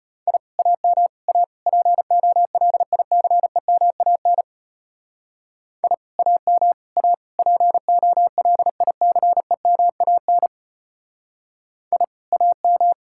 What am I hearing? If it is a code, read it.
Morse code, heard as a keyed tone, about 31 wpm, 700 Hz. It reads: I AM A POLICEMAN I AM A POLICEMAN I AM
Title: Speed-Racing Common QSO Phrases - 25wpm